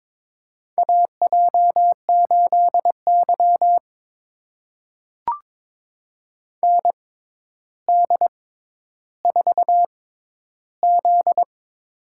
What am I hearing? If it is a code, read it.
Morse code, heard as a keyed tone, about 22 wpm, spaced as for 11 wpm, 700 Hz. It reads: AJ8Y ND4Z
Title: USA Call Signs - 4 Characters - 6x Character Spacing - 22wpm